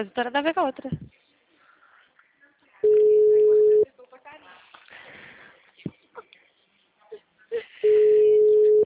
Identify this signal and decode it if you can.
Ringback tone (Europe, 425 Hz), with speech.
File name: out- -2003- - - .120